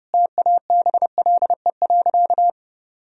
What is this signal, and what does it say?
Morse code, heard as a keyed tone, about 30 wpm, 700 Hz. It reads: TABLE.
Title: Sentences from Top 1000 Words - Review Part 4 - 30wpm